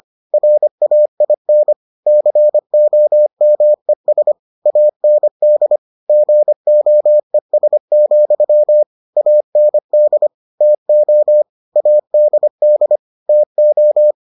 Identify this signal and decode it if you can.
Morse code, heard as a keyed tone, about 25 wpm, 600 Hz. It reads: RAIN COMES AND GOES, AND TO ADD TO